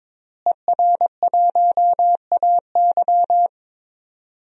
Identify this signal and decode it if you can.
Morse code, heard as a keyed tone, about 22 wpm, 700 Hz. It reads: ER1AY
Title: DXCC-100 Call Signs - 5 Characters - Repeated 2x - 22wpm